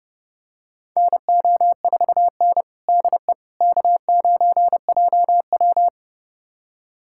Morse code, heard as a keyed tone, about 30 wpm, 700 Hz. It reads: NO4N DE K9JW
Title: USA Callsign DE Callsign - 4 Characters - 30wpm